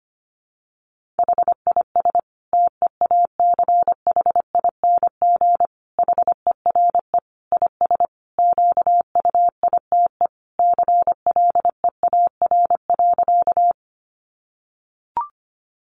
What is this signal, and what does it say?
Morse code, heard as a keyed tone, about 25 wpm, 700 Hz. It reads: HIS TEACHING HERE IS QUITE CLEAR.
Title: Sentences from Top 1100 Words 25wpm